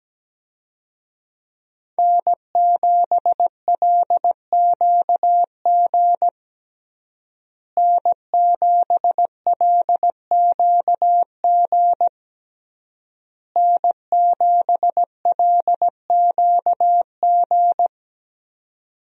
Morse code, heard as a keyed tone, about 17 wpm, 700 Hz. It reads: N7LQG N7LQG N7LQG
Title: USA Call Signs - 5 Characters - Repeated 3x - 17wpm